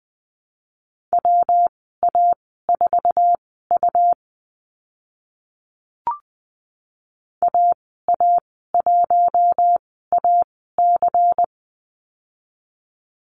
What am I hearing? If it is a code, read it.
Morse code, heard as a keyed tone, about 20 wpm, spaced as for 14 wpm, 700 Hz. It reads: WA4U AA1AC
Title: 154 Lesson 25g - Letter p - Call Signs 20wpm